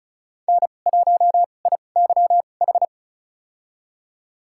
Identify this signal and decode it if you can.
Morse code, heard as a keyed tone, about 36 wpm, spaced as for 25 wpm, 700 Hz. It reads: N1IYH
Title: USA Call Signs - 2x Character Spacing - 35wpm